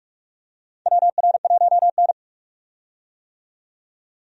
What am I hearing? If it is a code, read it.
Morse code, heard as a keyed tone, about 46 wpm, 700 Hz. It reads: WR1N
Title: USA Call Signs - 4 Characters - 45wpm